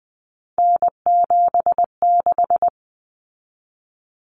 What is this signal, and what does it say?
Morse code, heard as a keyed tone, about 20 wpm, 700 Hz. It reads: N76
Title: Common Cut Numbers - 3-Digit 20wpm